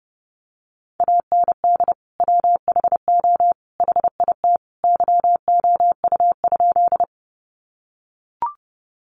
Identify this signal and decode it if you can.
Morse code, heard as a keyed tone, about 30 wpm, 700 Hz. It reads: AND WHO HIT YOU?
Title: Sentences from Top 1200 Words 30wpm